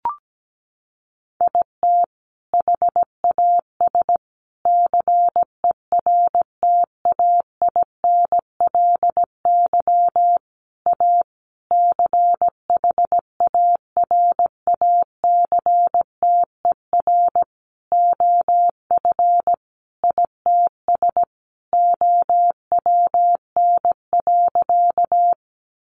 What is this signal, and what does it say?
Morse code, heard as a keyed tone, about 17 wpm, 700 Hz. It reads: IT HAS CERTAINLY A CHARACTER OF ITS OWN.
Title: Sentences from Top 1200 Words 17wpm